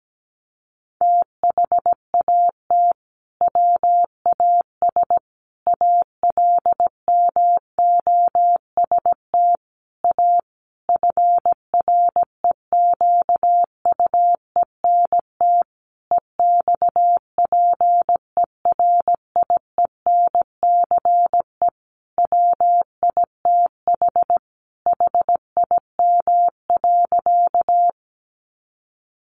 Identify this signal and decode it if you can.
Morse code, heard as a keyed tone, about 17 wpm, 700 Hz. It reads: THAT WAS ALMOST A FREQUENT EXPERIENCE WITH HIM.
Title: Sentences from Top 1400 Words 17wpm